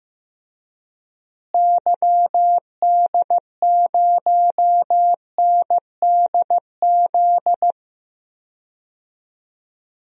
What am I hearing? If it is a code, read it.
Morse code, heard as a keyed tone, about 15 wpm, 700 Hz. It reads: YD0NDZ
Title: DXCC-100 Call Signs - 6 Characters - Repeated 3x - 15wpm